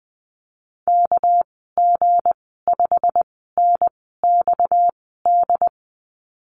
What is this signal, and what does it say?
Morse code, heard as a keyed tone, about 20 wpm, spaced as for 14 wpm, 700 Hz. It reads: KG5NXD
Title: 231 Lesson 37g - Letter x - Call Signs 20wpm